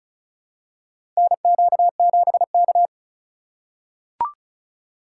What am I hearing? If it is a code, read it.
Morse code, heard as a keyed tone, about 36 wpm, 700 Hz. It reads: NQ7K
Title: USA Call Signs - 2x Character Spacing - 35wpm